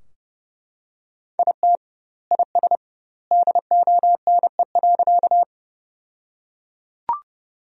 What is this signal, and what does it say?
Morse code, heard as a keyed tone, about 30 wpm, 700 Hz. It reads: IT IS DONE.
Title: 2x Word-Spacing - Sentences from Top 500 Words - 30wpm